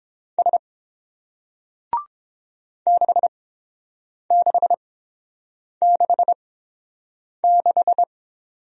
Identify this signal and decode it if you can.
Morse code, heard as a keyed tone, about 30 wpm, spaced as for 7 wpm, 700 Hz. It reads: S 6666